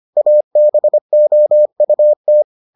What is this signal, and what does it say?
Morse code, heard as a keyed tone, about 25 wpm, 600 Hz. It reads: ABOUT